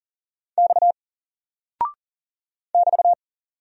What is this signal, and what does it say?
Morse code, heard as a keyed tone, about 40 wpm, 700 Hz. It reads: X <BT>
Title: Single Character (Letter:Number:Punctuation:Prosign) 40wpm